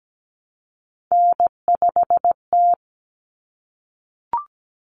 Morse code, heard as a keyed tone, about 17 wpm, 700 Hz. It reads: N5T
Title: Common Cut Numbers - 3-Digit 17wpm